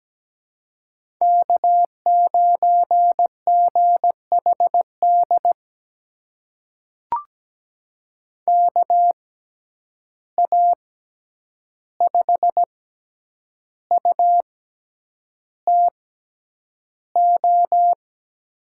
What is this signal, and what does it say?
Morse code, heard as a keyed tone, about 17 wpm, spaced as for 8 wpm, 700 Hz. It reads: K9GHD KA5UTO
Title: USA Call Signs - 6x Character Spacing - 17wpm